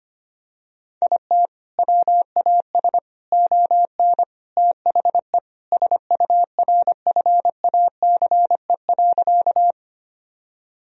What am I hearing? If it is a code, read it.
Morse code, heard as a keyed tone, about 25 wpm, 700 Hz. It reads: IT WAS ON THE SURFACE.
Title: Sentences from Top 1000 Words - Review Part 3 - 25wpm